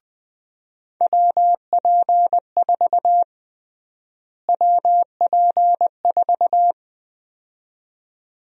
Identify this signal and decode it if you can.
Morse code, heard as a keyed tone, about 20 wpm, 700 Hz. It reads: WP4 WP4
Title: USA Call Sign Prefixes - Repeated 2x - 20wpm